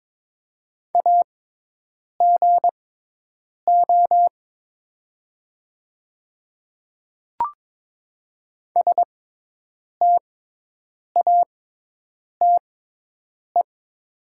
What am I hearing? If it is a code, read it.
Morse code, heard as a keyed tone, about 22 wpm, spaced as for 8 wpm, 700 Hz. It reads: AGO STATE
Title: 6x Character Spacing - Top 500 Words - 22wpm